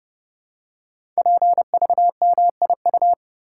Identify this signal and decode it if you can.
Morse code, heard as a keyed tone, about 31 wpm, 700 Hz. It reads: PVMIU
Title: Code Groups - 5 Letter:Numbers - 30wpm